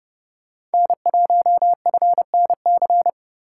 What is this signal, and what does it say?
Morse code, heard as a keyed tone, about 30 wpm, 700 Hz. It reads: N1FNC
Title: USA Call Signs - 5 Characters - 3x Character Spacing - 30wpm